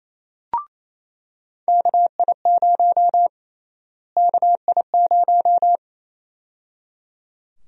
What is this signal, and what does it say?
Morse code, heard as a keyed tone, about 28 wpm, 700 Hz. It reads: KI0 KI0
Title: USA Call Sign Prefixes - Repeated 2x - 28wpm